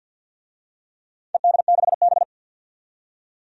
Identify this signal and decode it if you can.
Morse code, heard as a keyed tone, about 46 wpm, 700 Hz. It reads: ED6B
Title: DXCC-100 Call Signs - 4 Characters - 50wpm